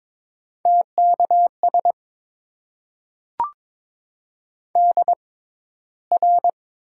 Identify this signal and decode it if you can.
Morse code, heard as a keyed tone, about 22 wpm, spaced as for 14 wpm, 700 Hz. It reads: TKS DR
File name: Top 90 QSO Elements - 6x Character Spacing - 22wpm